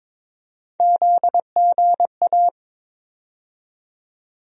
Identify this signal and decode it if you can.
Morse code, heard as a keyed tone, about 22 wpm, 700 Hz. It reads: ZGA
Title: USA Call Sign Postfixes - 3 Letters 22wpm